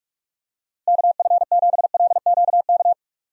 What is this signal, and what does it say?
Morse code, heard as a keyed tone, about 41 wpm, 700 Hz. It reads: KF7LQK
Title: USA Call Signs - 6 Characters - Repeated 3x - 45wpm